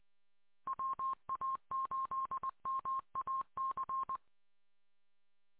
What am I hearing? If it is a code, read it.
Morse code, heard as a keyed tone, about 20 wpm, 1060 Hz. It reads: WA8MAC